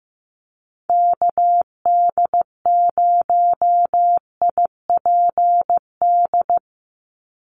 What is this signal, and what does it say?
Morse code, heard as a keyed tone, about 15 wpm, 700 Hz. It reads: KD0IPD
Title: USA Call Signs 15wpm